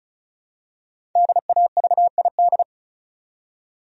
Morse code, heard as a keyed tone, about 34 wpm, 700 Hz. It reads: DAVID
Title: USA Common Names - 35wpm